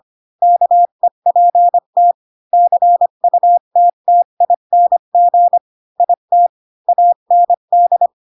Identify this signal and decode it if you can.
Morse code, heard as a keyed tone, about 25 wpm, 700 Hz. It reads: KEPT CUTTING IT AND
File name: Pinocchio / Pinocchio ch3 20 25 ik1hge